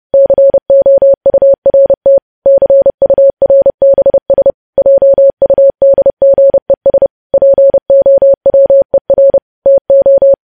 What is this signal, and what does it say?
Morse code, heard as a keyed tone, about 30 wpm, 550 Hz. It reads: COURT CURBS JUDGES POWER TO
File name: podcast / News headlines-1x-30x30